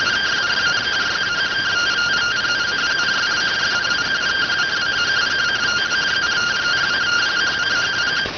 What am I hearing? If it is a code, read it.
Two-tone teleprinter signal, about 170 Hz shift.